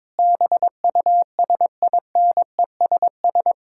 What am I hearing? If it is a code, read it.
Morse code, heard as a keyed tone, about 21 wpm, 700 Hz. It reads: BUSINESS
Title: Sentences from Top 1000 Words - Review Part 6 - 22wpm